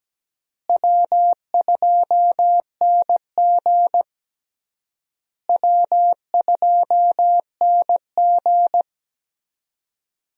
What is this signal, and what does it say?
Morse code, heard as a keyed tone, about 17 wpm, 700 Hz. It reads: W2NG W2NG
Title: USA Call Signs - 4 Characters - Repeated 3x - 17wpm